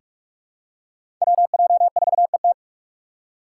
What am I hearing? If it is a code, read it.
Morse code, heard as a keyed tone, about 42 wpm, 700 Hz. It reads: WJ4ET
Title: USA Call Signs - 5 Characters - 4x Character Spacing - 45wpm